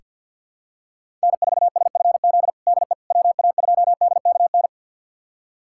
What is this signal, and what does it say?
Morse code, heard as a keyed tone, about 45 wpm, 700 Hz. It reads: N4SFZ DE WA2DKN
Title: USA Callsign DE Callsign 50wpm